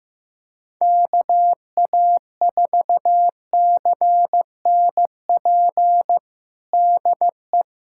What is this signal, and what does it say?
Morse code, heard as a keyed tone, about 15 wpm, 700 Hz. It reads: KA4CNP DE
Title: USA Callsign DE Callsign - 6 Characters - 15wpm